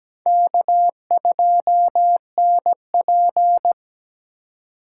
Morse code, heard as a keyed tone, about 21 wpm, 700 Hz. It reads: K2NP
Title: Speed Racing - USA Call Signs 17wpm